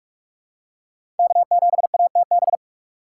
Morse code, heard as a keyed tone, about 46 wpm, 700 Hz. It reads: K7ATB
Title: USA Call Signs - 2x Character Spacing - 45wpm